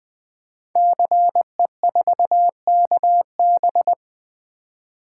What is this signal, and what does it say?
Morse code, heard as a keyed tone, about 20 wpm, 700 Hz. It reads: CE4KB
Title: Code Groups - 5 Letter:Numbers - 20wpm